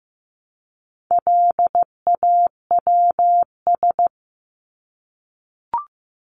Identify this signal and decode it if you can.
Morse code, heard as a keyed tone, about 15 wpm, 700 Hz. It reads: LAWS
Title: ES:S Suffix - Top 100 Words with Base Words 15wpm